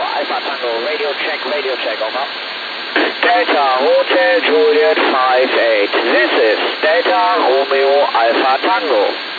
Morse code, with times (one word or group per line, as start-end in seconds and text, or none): none